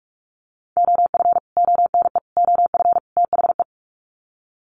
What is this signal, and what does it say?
Morse code, heard as a keyed tone, about 44 wpm, 700 Hz. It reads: OF ONE OF THE
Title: Sets of 4 Words 45wpm